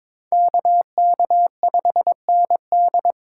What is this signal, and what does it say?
Morse code, heard as a keyed tone, about 22 wpm, 700 Hz. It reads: KK5ND
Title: USA Call Signs - 5 Characters - 3x Character Spacing - 22wpm